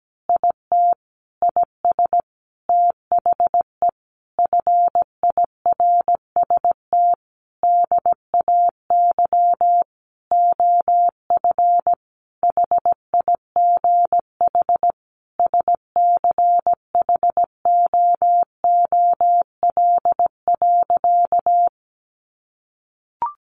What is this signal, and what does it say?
Morse code, heard as a keyed tone, about 17 wpm, 700 Hz. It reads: IT IS THE FIRST DAY OF HIGH SCHOOL.
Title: Additional Easy Phrases 17wpm